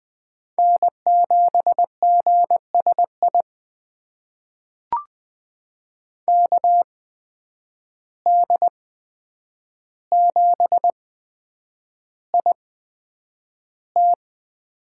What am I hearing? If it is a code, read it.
Morse code, heard as a keyed tone, about 20 wpm, spaced as for 9 wpm, 700 Hz. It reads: N7GSI KD7IT
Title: USA Call Signs - 5 Characters - 8x Character Spacing - 20wpm